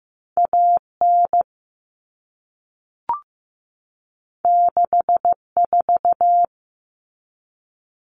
Morse code, heard as a keyed tone, about 15 wpm, 700 Hz. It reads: AN 64
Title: Common Cut Numbers - 2-Digit 15wpm